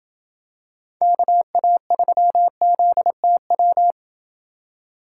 Morse code, heard as a keyed tone, about 27 wpm, 700 Hz. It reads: KA3ZTW